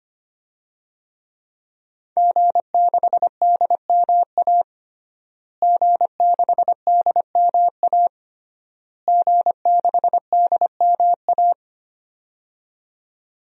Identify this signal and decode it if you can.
Morse code, heard as a keyed tone, about 25 wpm, 700 Hz. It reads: G6DMA G6DMA G6DMA
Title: DXCC-100 Call Signs - 5 Characters - Repeated 3x - 25wpm